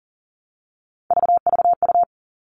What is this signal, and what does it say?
Morse code, heard as a keyed tone, about 40 wpm, 700 Hz. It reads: VVU
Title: All Cut Numbers - 3-Digit 40wpm